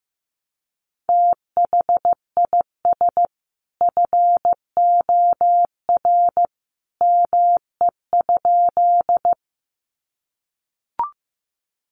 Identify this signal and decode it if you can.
Morse code, heard as a keyed tone, about 15 wpm, 700 Hz. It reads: THIS FOR ME?
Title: Sentences from Top 200 Words 15wpm